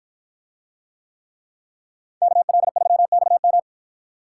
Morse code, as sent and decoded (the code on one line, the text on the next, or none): -.- .-.. ...-- -..- --
KL3XM